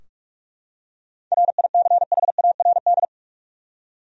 Morse code, heard as a keyed tone, about 46 wpm, 700 Hz. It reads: RICHARD
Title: USA Common Names - 45wpm